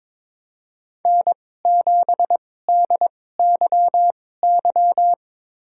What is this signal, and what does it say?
Morse code, heard as a keyed tone, about 21 wpm, spaced as for 16 wpm, 700 Hz. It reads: N7DYY
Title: USA Call Signs - 5 Characters - 2x Character Spacing - 22wpm